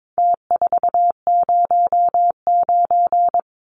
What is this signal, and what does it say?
Morse code, heard as a keyed tone, about 22 wpm, 700 Hz. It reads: T409
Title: Numbers - Contest Style - 4-Digit 22wpm